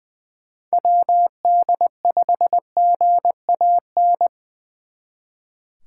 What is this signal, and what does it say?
Morse code, heard as a keyed tone, about 24 wpm, 700 Hz. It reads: WD5GAN
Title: Speed Racing - USA 2x3 Call Signs 20wpm